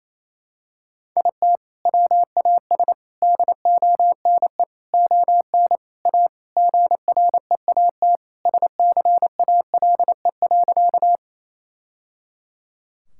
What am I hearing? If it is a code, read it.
Morse code, heard as a keyed tone, about 28 wpm, 700 Hz. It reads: IT WAS DONE ON A GREAT SCALE.
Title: Sentences from Top 1000 Words - Review Part 5 - 28wpm